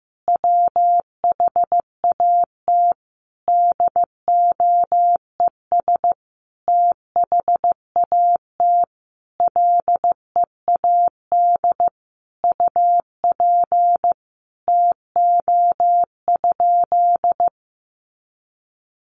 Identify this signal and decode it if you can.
Morse code, heard as a keyed tone, about 15 wpm, 700 Hz. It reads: WHAT DOES THAT LEAD UP TO?
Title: Sentences from Top 500 Words 15wpm